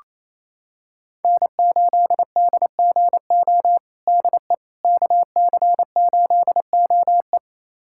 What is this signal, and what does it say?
Morse code, heard as a keyed tone, about 28 wpm, 700 Hz. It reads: N8DGO DE KC8OE
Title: USA Callsign DE Callsign - 5 Characters - 28wpm